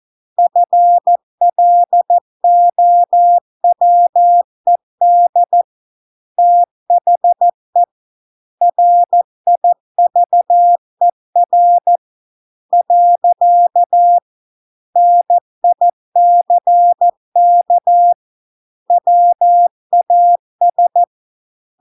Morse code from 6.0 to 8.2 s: THE